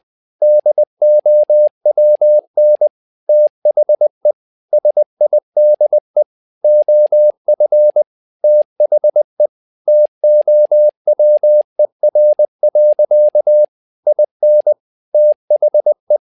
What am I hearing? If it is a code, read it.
Morse code, heard as a keyed tone, about 20 wpm, 600 Hz. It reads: DOWN THE SIDE OF THE TOWER. IN THE
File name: learning center / tower trouble / tower trouble 20wpm